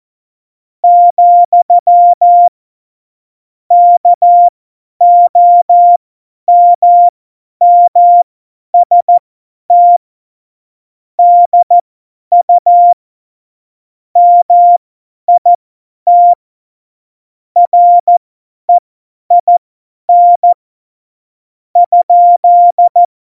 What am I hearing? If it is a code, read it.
Morse code, heard as a keyed tone, about 14 wpm, spaced as for 10 wpm, 700 Hz. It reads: , KOMMST DU MIT REIN ?
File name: Morsekurs / Witze / Witz 09. Oktober 2018, 50 F70